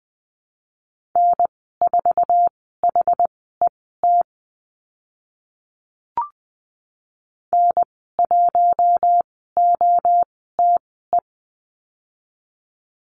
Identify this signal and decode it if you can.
Morse code, heard as a keyed tone, about 20 wpm, spaced as for 14 wpm, 700 Hz. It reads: N4HET N1OTE
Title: 076 Lesson 13g - Letter L - Call Signs 20wpm